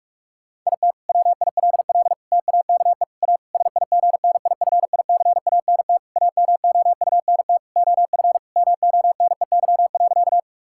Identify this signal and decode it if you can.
Morse code, heard as a keyed tone, about 43 wpm, 700 Hz. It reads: IT WILL TAKE A SIGNIFICANT AMOUNT OF MONEY.
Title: Sentences from Top 1000 Words - Review Part 6 - 45wpm